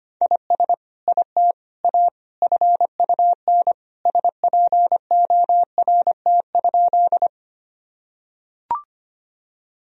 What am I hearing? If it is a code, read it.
Morse code, heard as a keyed tone, about 25 wpm, 700 Hz. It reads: IS IT A FUN SPORT?
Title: Sentences from Top 1000 Words - Review Part 4 - 25wpm